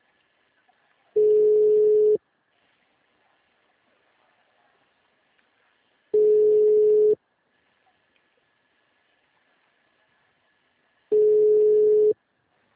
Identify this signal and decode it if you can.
Ringback tone (Europe, 425 Hz).
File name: out- -2008- - - .16